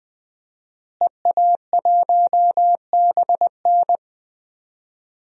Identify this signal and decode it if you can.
Morse code, heard as a keyed tone, about 20 wpm, 700 Hz. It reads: EA1BN
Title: DXCC-100 Call Signs - 5 Characters - Repeated 2x - 20wpm